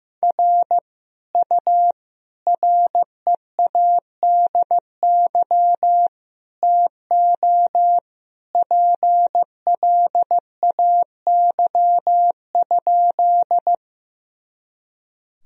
Morse code, heard as a keyed tone, about 15 wpm, 700 Hz. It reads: R U READY TO PLAY?
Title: Sentences from Top 400 Words and Abbreviations 15wpm